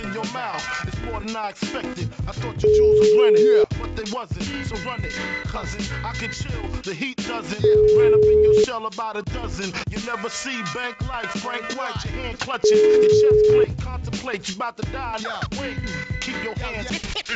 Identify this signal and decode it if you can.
Ringback tone (Europe, 425 Hz), with speech.